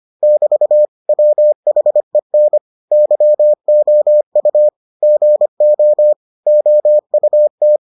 Morse code, heard as a keyed tone, about 25 wpm, 600 Hz. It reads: <BT> WHEN YOU GO OUT